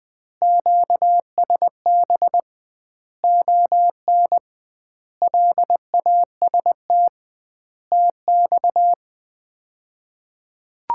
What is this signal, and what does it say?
Morse code, heard as a keyed tone, about 20 wpm, 700 Hz. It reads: QSB ON LAST TX
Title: Common QSO Phrases - 2x Word-Spacing - 20wpm